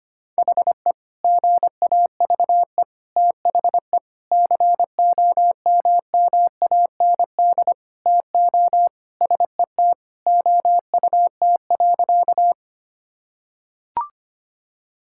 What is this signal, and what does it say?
Morse code, heard as a keyed tone, about 25 wpm, 700 Hz. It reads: HE GAVE THE COMMAND TO SET OUT.
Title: Sentences from Top 1600 Words 25wpm